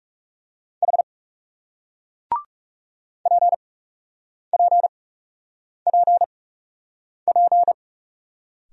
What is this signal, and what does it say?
Morse code, heard as a keyed tone, about 41 wpm, spaced as for 8 wpm, 700 Hz. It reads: H PPPP